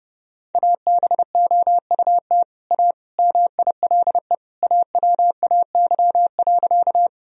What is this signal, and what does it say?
Morse code, heard as a keyed tone, about 30 wpm, 700 Hz. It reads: ABOUT A MILE AWAY.
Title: Sentences from Top 1000 Words - Review Part 1 - 30wpm